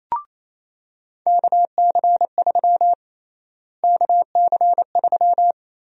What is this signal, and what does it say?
Morse code, heard as a keyed tone, about 28 wpm, 700 Hz. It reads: KC3 KC3
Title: USA Call Sign Prefixes - Repeated 2x - 28wpm